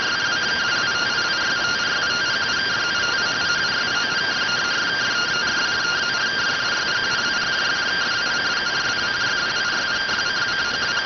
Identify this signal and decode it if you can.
Two-tone teleprinter signal, about 170 Hz shift.